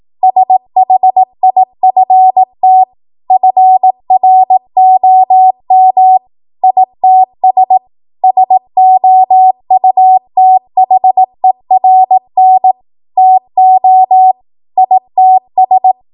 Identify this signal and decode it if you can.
Morse code, heard as a keyed tone, about 18 wpm, 750 Hz. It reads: SHIFT FROM ITS SOUTHERN TO ITS
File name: Morse / 18WPM